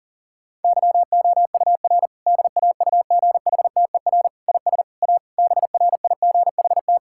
Morse code, heard as a keyed tone, about 37 wpm, 700 Hz. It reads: YOUR DAUGHTER IS A BRIGHT
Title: Sentences from Top 1600 Words 40wpm